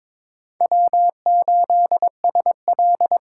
Morse code, heard as a keyed tone, about 26 wpm, 700 Hz. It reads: W8SL
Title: Speed Racing - USA 1x2 Call Signs 22wpm